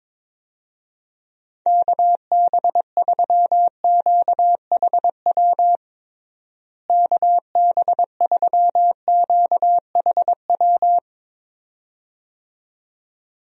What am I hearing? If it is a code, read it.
Morse code, heard as a keyed tone, about 22 wpm, 700 Hz. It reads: KB3QHW KB3QHW
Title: USA Call Signs - 6 Characters - Repeated 2x - 22wpm